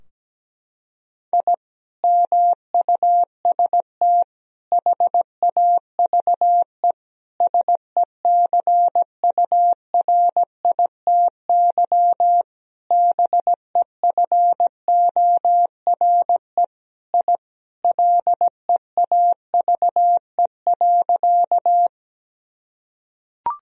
Morse code, heard as a keyed tone, about 17 wpm, 700 Hz. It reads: I MUST HAVE SECURITY BEFORE I LEAVE.
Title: Sentences from Top 1000 Words - Review Part 4 - 17wpm